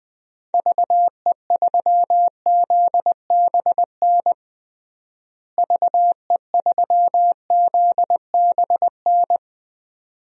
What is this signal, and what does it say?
Morse code, heard as a keyed tone, about 20 wpm, 700 Hz. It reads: VE3ZBN VE3ZBN
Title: DXCC-100 Call Signs - 6 Characters - Repeated 2x - 20wpm